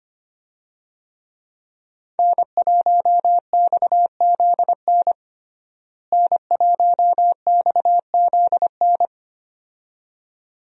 Morse code, heard as a keyed tone, about 25 wpm, 700 Hz. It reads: N1XZN N1XZN
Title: USA Call Signs - 5 Characters - Repeated 2x - 25wpm